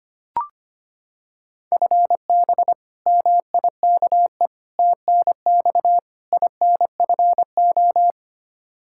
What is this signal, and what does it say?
Morse code, heard as a keyed tone, about 25 wpm, 700 Hz. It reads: FB MIKE TNX INFO
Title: Practice QSO - Exchange 3 - More Info - 25wpm